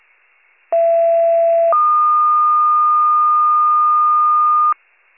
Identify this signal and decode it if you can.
Two-tone page, heard as a one-second tone, then a three-second tone, about 670 then 1220 Hz. It is heard from 0.7 to 4.7 s.